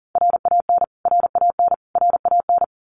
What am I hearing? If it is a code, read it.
Morse code, heard as a keyed tone, about 40 wpm, 700 Hz. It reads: RAN RAN RAN
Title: Top 500 Words Repeated 40wpm